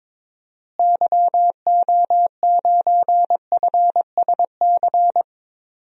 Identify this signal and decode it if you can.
Morse code, heard as a keyed tone, about 23 wpm, 700 Hz. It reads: YO9FSC
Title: DXCC-100 Call Signs - 6 Characters - Repeated 2x - 22wpm